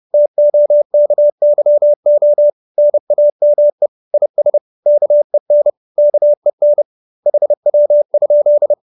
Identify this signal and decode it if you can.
Morse code, heard as a keyed tone, about 30 wpm, 590 Hz. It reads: TOKYO NAME IS KEN KEN HW?